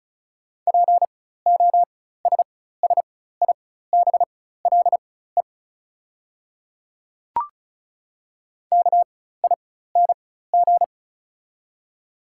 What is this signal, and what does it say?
Morse code, heard as a keyed tone, about 36 wpm, spaced as for 16 wpm, 690 Hz. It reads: POSSIBLE KING
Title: 4x Character Spacing - Top 500 Words - 35wpm